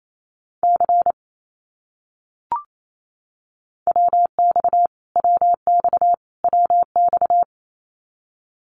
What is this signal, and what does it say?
Morse code, heard as a keyed tone, about 28 wpm, 700 Hz. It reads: C WX WX WX
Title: Abbreviations Repeated 3x - 28wpm